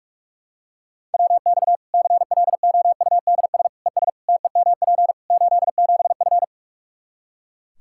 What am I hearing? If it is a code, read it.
Morse code, heard as a keyed tone, about 43 wpm, 700 Hz. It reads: WX CLOUDS ES TEMP 87F
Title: Practice QSO - Exchange 2 - Info - 45wpm